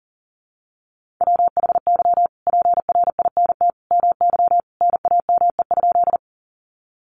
Morse code, heard as a keyed tone, about 40 wpm, 700 Hz. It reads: WHY PRINT MY NAME?
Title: Sentences from Top 1400 Words 40wpm